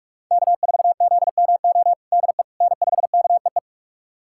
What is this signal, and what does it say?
Morse code, heard as a keyed tone, about 43 wpm, 700 Hz. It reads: K4ZMO DE N5KEE
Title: USA Callsign DE Callsign - 5 Characters - 45wpm